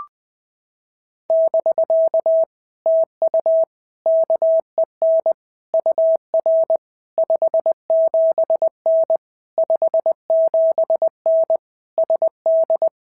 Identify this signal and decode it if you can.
Morse code, heard as a keyed tone, about 19 wpm, 650 Hz. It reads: <BK> TU KEN UR 57N 57N SD
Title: POTA QSO Practice - 20wpm